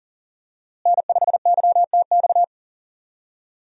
Morse code, heard as a keyed tone, about 40 wpm, 700 Hz. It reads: N5YTX
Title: USA Call Signs - 5 Characters - Repeated 3x - 40wpm